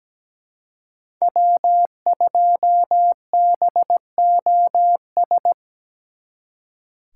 Morse code, heard as a keyed tone, about 20 wpm, 700 Hz. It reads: W2BOS
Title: Speed Racing - USA 1x3 Call Signs 17wpm